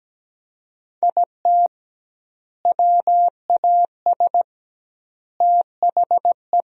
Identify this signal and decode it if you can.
Morse code, heard as a keyed tone, about 17 wpm, 700 Hz. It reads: IT WAS THE
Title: 2x Word-Spacing - Sentences from Top 400 Words - 17wpm